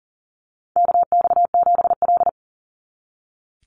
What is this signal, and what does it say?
Morse code, heard as a keyed tone, about 40 wpm, 700 Hz. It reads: KX7L
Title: NA SOTA Chaser Callsigns 40wpm